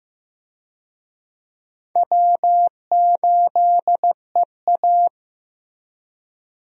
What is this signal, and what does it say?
Morse code, heard as a keyed tone, about 16 wpm, 700 Hz. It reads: W8EA
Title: USA Call Signs - 4 Characters - Repeated 3x - 15wpm